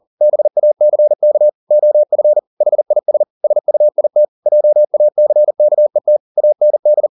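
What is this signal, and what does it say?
Morse code, heard as a keyed tone, about 40 wpm, 600 Hz. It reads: BACK OF HIS SUIT JACKET AND
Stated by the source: Morse code at 40 wpm, practice text, CW med Gnister.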